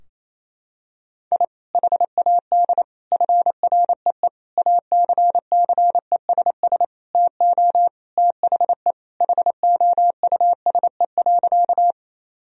Morse code, heard as a keyed tone, about 28 wpm, 700 Hz. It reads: I HAD FREE ACCESS TO THE HOUSE.
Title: Sentences from Top 1000 Words - Review Part 4 - 28wpm